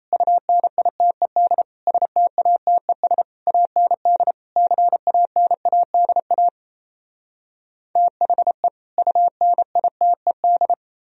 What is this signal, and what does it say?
Morse code, heard as a keyed tone, about 38 wpm, 700 Hz. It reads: UNITED STATES AND CANADA THE UNITED